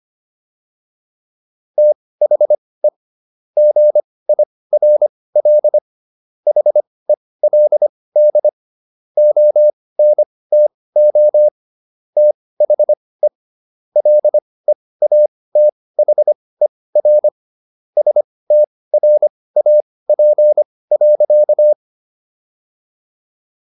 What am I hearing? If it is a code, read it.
Morse code, heard as a keyed tone, about 25 wpm, spaced as for 18 wpm, 600 Hz. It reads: THE GIRL HELD ONTO THE LEATHER STRAP.